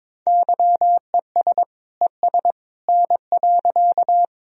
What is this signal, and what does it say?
Morse code, heard as a keyed tone, about 21 wpm, 700 Hz. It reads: YES ES N.
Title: Sentences from Top 500 Words and Abbreviations 22wpm